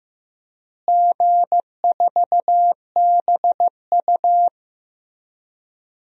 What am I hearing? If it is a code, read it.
Morse code, heard as a keyed tone, about 15 wpm, 700 Hz. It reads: G4BU
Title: DXCC-100 Call Signs - 4 Characters - Repeated 3x - 15wpm